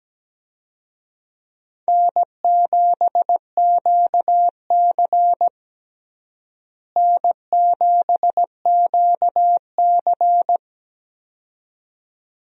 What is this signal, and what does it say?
Morse code, heard as a keyed tone, about 17 wpm, 700 Hz. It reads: N7QC N7QC
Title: USA Call Signs - 4 Characters - Repeated 2x - 17wpm